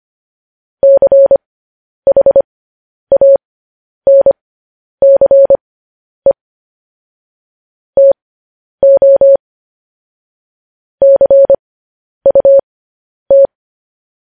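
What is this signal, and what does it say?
Morse code, heard as a keyed tone, about 25 wpm, spaced as for 10 wpm, 550 Hz. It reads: CHANCE TO CUT
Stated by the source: Morse code news headlines practice transmission.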